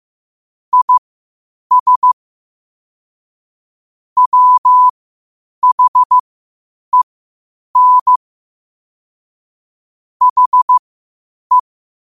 Morse code, heard as a keyed tone, about 15 wpm, spaced as for 8 wpm, 1000 Hz. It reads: IS WHEN HE
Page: Quotes for Fri, 15 Aug 2025 in Morse Code at 8 words per minute.